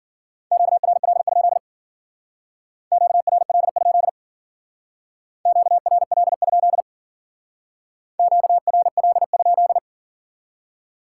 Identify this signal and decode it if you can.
Morse code, heard as a keyed tone, about 52 wpm, 700 Hz. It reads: QRL? QRL? QRL? QRL?